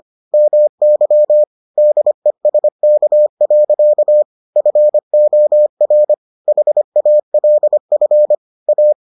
Morse code, heard as a keyed tone, about 25 wpm, 600 Hz. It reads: MY DESK. FOR HALF A